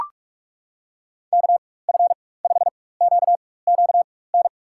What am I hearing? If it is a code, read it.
Morse code, heard as a keyed tone, about 42 wpm, spaced as for 20 wpm, 700 Hz. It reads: KF5QQN
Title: USA Call Signs - 6 Characters - 4x Character Spacing - 45wpm